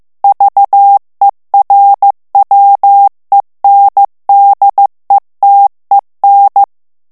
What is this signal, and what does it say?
Morse code, heard as a keyed tone, about 15 wpm, 800 Hz. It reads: VERWENDETEN
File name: morse mp3 / ALLEMAND texte clair 1 vitesse 14 mots minute